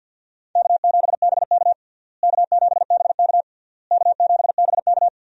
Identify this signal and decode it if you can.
Morse code, heard as a keyed tone, about 47 wpm, 700 Hz. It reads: K7BK K7BK K7BK
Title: USA Call Signs - 4 Characters - Repeated 3x - 50wpm